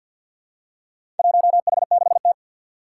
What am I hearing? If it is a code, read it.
Morse code, heard as a keyed tone, about 47 wpm, 700 Hz. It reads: 1H6T
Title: Four Challenging Letter-Number Combinations 50wpm